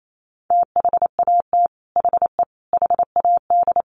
Morse code, heard as a keyed tone, about 28 wpm, 700 Hz. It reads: THAT HE HAD
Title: Sets of 3 Words 28wpm